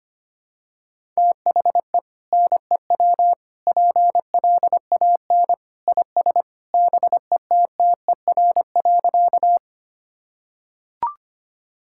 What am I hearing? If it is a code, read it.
Morse code, heard as a keyed tone, about 25 wpm, 700 Hz. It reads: THE NEW PLAN IS BETTER.
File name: practice / 25wpm / Sentences with 2-Word Phrases - 25wpm